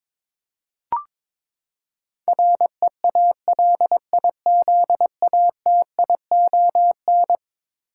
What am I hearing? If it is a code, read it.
Morse code, heard as a keyed tone, about 22 wpm, 700 Hz. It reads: REALIZATION
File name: practice / 22wpm / Eleven Letter Words - 22wpm